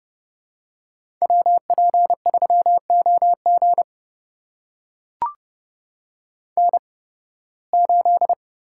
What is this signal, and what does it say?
Morse code, heard as a keyed tone, about 30 wpm, spaced as for 20 wpm, 700 Hz. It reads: WP3OG N8
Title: USA Call Signs - 5 Characters - 8x Character Spacing - 30wpm